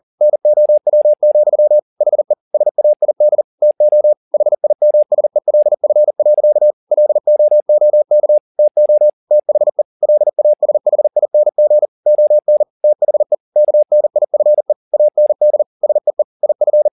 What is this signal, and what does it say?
Morse code, heard as a keyed tone, about 40 wpm, 600 Hz. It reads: NOW, HE SAID TO HIMSELF. LOOK TO THE LASHING ON THE KNIFE AND SEE IF
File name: the old man and the sea 2 40wpm